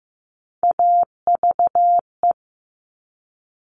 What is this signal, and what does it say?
Morse code, heard as a keyed tone, about 15 wpm, 700 Hz. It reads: AVE
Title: Top Three Letter Combinations in Words 15wpm